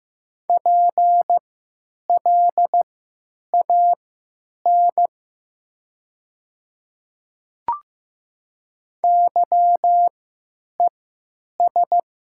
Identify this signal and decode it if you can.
Morse code, heard as a keyed tone, about 15 wpm, spaced as for 9 wpm, 700 Hz. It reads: PLAN YES
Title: 3x Character Spacing - Top 500 Words - 15wpm